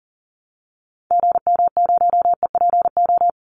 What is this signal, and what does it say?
Morse code, heard as a keyed tone, about 40 wpm, 700 Hz. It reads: GM0EPO
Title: Call Signs Repeated 2x - 40wpm